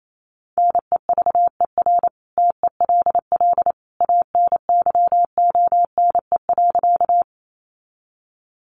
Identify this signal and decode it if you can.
Morse code, heard as a keyed tone, about 28 wpm, 700 Hz. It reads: NEVER TELL ANYONE.
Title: Sentences from Top 900 Words 28wpm